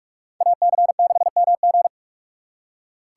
Morse code, heard as a keyed tone, about 41 wpm, 700 Hz. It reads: AC6MG
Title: USA Call Signs - 5 Characters - 4x Character Spacing - 45wpm